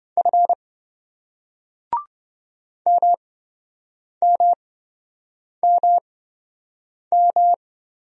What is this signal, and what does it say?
Morse code, heard as a keyed tone, about 25 wpm, spaced as for 7 wpm, 700 Hz. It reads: F MMMM